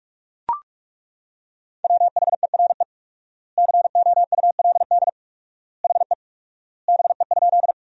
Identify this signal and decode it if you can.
Morse code, heard as a keyed tone, about 42 wpm, 700 Hz. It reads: WHERE COULD HE BE?
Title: 4x Word-Spacing - Sentences from Top 200 Words - 45wpm